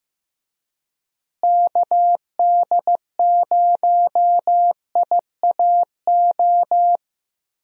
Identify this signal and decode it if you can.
Morse code, heard as a keyed tone, about 15 wpm, 700 Hz. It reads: KD0IAO
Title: USA Call Signs - 6 Characters - Repeated 3x - 15wpm